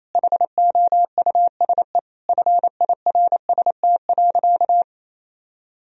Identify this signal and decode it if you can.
Morse code, heard as a keyed tone, about 27 wpm, 700 Hz. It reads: HOUSE FIRST.
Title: Sentences from Top 1000 Words - Review Part 3 - 28wpm